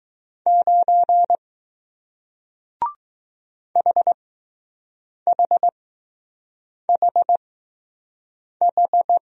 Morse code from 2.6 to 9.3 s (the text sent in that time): HHHH